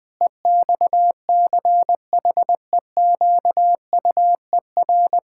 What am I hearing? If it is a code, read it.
Morse code, heard as a keyed tone, about 20 wpm, 700 Hz. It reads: EXCHEQUER
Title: Pangram Challenge v2 - 20wpm